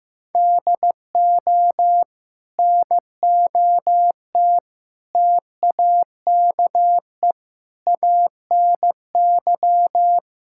Morse code, heard as a keyed tone, about 16 wpm, 700 Hz. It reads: DO NOT TAKE ANY
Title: Sentences from Top 1000 Words - Review Part 5 - 15wpm